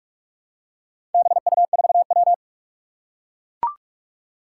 Morse code, heard as a keyed tone, about 41 wpm, 700 Hz. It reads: BU4W
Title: Four Challenging Letter-Number Combinations 45wpm